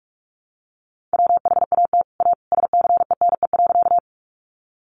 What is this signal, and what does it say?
Morse code, heard as a keyed tone, about 44 wpm, 700 Hz. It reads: WHAT A SCENE.
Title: Sentences from Top 1100 Words 45wpm